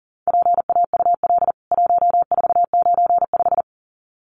Morse code, heard as a keyed tone, about 40 wpm, 700 Hz. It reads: PAUL 1495
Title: CWT Exchange 40wpm